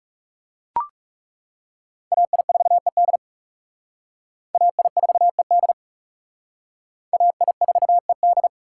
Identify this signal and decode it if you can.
Morse code, heard as a keyed tone, about 37 wpm, 700 Hz. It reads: AI4ED AI4ED AI4ED
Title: Speed Racing - USA 2x2 Call Signs 30wpm